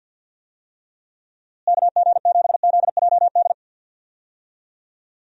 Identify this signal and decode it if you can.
Morse code, heard as a keyed tone, about 50 wpm, 700 Hz. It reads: KG7ZJD
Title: USA Call Signs - 6 Characters - Repeated 3x - 50wpm